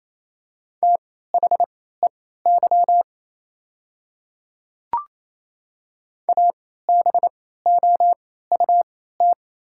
Morse code, heard as a keyed tone, about 28 wpm, spaced as for 16 wpm, 690 Hz. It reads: THEY ABOUT
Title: 3x Character Spacing - Top 100 Words - 28wpm